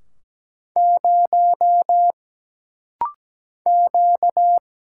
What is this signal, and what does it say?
Morse code, heard as a keyed tone, about 17 wpm, 700 Hz. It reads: 0 Q
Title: Single Letter-Number Alt - Rapid-Fire 17wpm